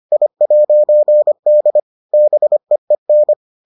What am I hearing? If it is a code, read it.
Morse code, heard as a keyed tone, about 25 wpm, 600 Hz. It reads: I'D BEEN